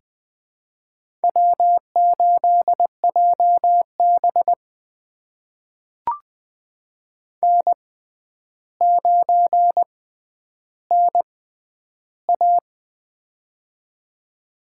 Morse code, heard as a keyed tone, about 20 wpm, spaced as for 10 wpm, 700 Hz. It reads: W8JB N9NA
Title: USA Call Signs - 4 Characters - 6x Character Spacing - 20wpm